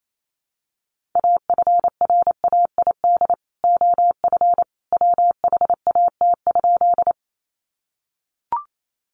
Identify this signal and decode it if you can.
Morse code, heard as a keyed tone, about 28 wpm, 700 Hz. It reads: AFRAID OF WHAT?
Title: Sentences from Top 1500 Words 28wpm